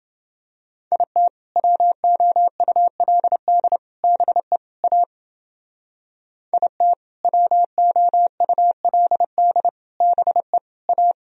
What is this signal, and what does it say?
Morse code, heard as a keyed tone, about 29 wpm, 700 Hz. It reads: IT WOULD BE A IT WOULD BE A